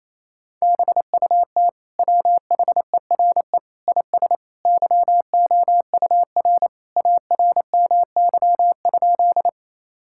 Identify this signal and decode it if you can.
Morse code, heard as a keyed tone, about 27 wpm, 700 Hz. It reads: BUT WHERE IS YOUR ARMY?
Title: Sentences from Top 1000 Words - Review Part 5 - 28wpm